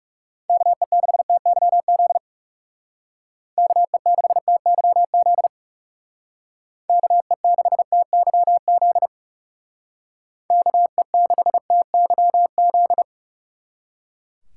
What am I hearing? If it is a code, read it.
Morse code, heard as a keyed tone, about 39 wpm, 700 Hz. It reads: KE6TYZ KE6TYZ KE6TYZ KE6TYZ